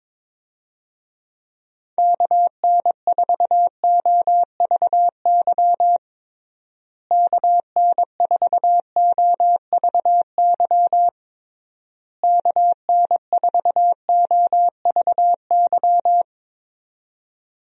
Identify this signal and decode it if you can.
Morse code, heard as a keyed tone, about 22 wpm, 700 Hz. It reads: KN4OVY KN4OVY KN4OVY
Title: USA Call Signs - 6 Characters - Repeated 3x - 22wpm